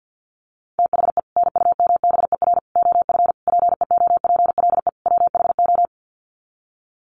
Morse code, heard as a keyed tone, about 50 wpm, 700 Hz. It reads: THE NUMBER OF PEOPLE WHO
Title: Sets of 5 Words Encore 50wpm